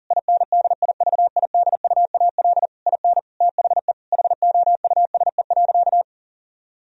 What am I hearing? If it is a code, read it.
Morse code, heard as a keyed tone, about 37 wpm, 700 Hz. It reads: INDIVIDUAL IN THE HOUSE.
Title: Sentences from Top 1000 Words - Review Part 4 - 40wpm